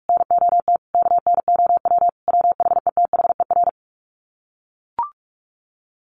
Morse code, heard as a keyed tone, about 44 wpm, 700 Hz. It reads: NOT KNOW WHETHER
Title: Sets of 5 Words 45wpm